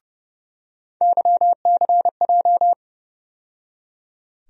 Morse code, heard as a keyed tone, about 31 wpm, 700 Hz. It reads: YCJ
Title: USA Call Sign Postfixes - 3 Letters 30wpm